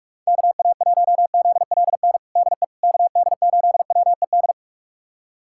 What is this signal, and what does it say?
Morse code, heard as a keyed tone, about 46 wpm, 700 Hz. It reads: KA1ZLN DE KD8WED
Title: USA Callsign DE Callsign - 6 Characters - 45wpm